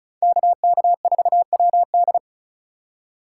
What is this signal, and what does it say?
Morse code, heard as a keyed tone, about 36 wpm, 700 Hz. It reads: KK4WD
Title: USA Call Signs - 5 Characters - 3x Character Spacing - 35wpm